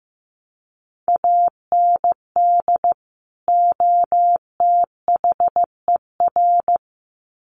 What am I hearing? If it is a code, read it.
Morse code, heard as a keyed tone, about 15 wpm, 700 Hz. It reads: AND OTHER
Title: Sets of 2 Words 15wpm